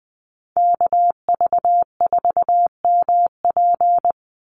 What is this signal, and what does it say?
Morse code, heard as a keyed tone, about 20 wpm, 700 Hz. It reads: KV4MP
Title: USA Call Signs 20wpm